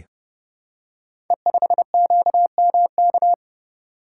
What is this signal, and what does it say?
Morse code, heard as a keyed tone, about 30 wpm, 700 Hz. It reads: E5QMK
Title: Code Groups - 5 Letter:Numbers - 30wpm